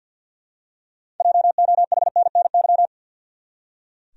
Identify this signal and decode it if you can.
Morse code, heard as a keyed tone, about 46 wpm, 700 Hz. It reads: JOHNNY
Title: USA Common Names - 50wpm